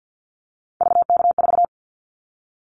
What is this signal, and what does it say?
Morse code, heard as a keyed tone, about 50 wpm, 700 Hz. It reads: VK4
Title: Callsign Prefixes for Top 50 DXCC - 50wpm